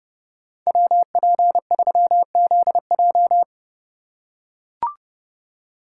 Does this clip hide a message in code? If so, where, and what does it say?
Yes: Morse code from 0.7 to 3.4 s, heard as a keyed tone, about 30 wpm, 700 Hz. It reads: WP3ZJ